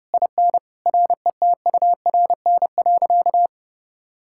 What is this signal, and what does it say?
Morse code, heard as a keyed tone, about 30 wpm, 700 Hz. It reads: IN RETURN.
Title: Sentences from Top 1000 Words - Review Part 6 - 30wpm